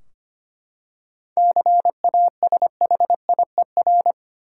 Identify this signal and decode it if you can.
Morse code, heard as a keyed tone, about 25 wpm, 700 Hz. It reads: CASHIER
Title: Practice QSO - Exchange 3 - More Info - 25wpm